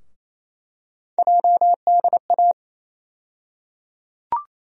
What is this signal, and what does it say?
Morse code, heard as a keyed tone, about 28 wpm, 700 Hz. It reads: JDA
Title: USA Call Sign Postfixes - 3 Letters 28wpm